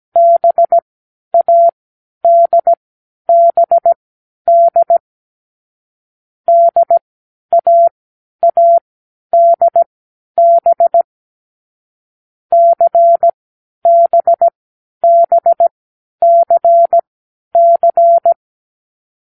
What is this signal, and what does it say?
Morse code, heard as a keyed tone, about 17 wpm, spaced as for 10 wpm, 680 Hz. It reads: BADBD DAADB CBBCC
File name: k7qo / Lesson6(A-D)